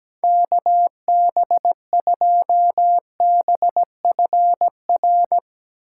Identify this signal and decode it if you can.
Morse code, heard as a keyed tone, about 19 wpm, 700 Hz. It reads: KB2BFR
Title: Speed Racing - USA 2x3 Call Signs 17wpm